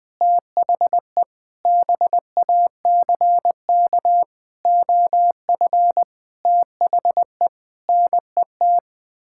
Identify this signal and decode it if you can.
Morse code, heard as a keyed tone, about 20 wpm, 700 Hz. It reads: THE BACK OF THE NET
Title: Sets of 9 Words - 20wpm